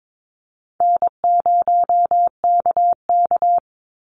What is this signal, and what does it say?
Morse code, heard as a keyed tone, about 22 wpm, 700 Hz. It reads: N0KK
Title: Easy Call Signs Repeated 3x - 22wpm